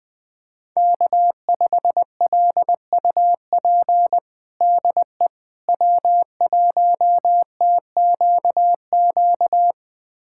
Morse code, heard as a keyed tone, about 20 wpm, 700 Hz. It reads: K5LUP DE W1TQQ
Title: USA Callsign DE Callsign - 5 Characters - 20wpm